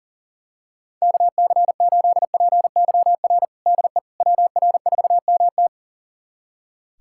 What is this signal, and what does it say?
Morse code, heard as a keyed tone, about 40 wpm, 700 Hz. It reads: KC8PYR DE WR4MT
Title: USA Callsign DE Callsign 40wpm